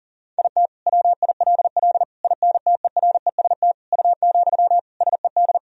Morse code, heard as a keyed tone, about 40 wpm, 700 Hz. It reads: IT WILL INTEREST U, SED
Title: Sentences from Top 400 Words and Abbreviations 40wpm